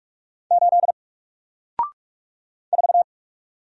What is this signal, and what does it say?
Morse code, heard as a keyed tone, about 45 wpm, 700 Hz. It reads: 8 4
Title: Single Letter-Number 45wpm